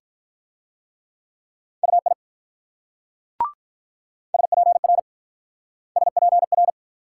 Morse code, heard as a keyed tone, about 48 wpm, 700 Hz. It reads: UI SPR SPR